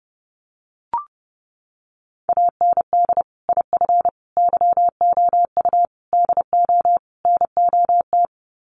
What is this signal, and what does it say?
Morse code, heard as a keyed tone, about 30 wpm, 700 Hz. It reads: AND IF YOU DO NOT
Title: Sets of 5 Words 30wpm